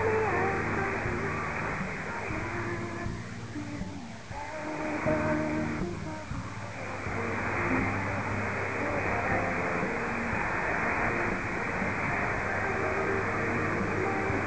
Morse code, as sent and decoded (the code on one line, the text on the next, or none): none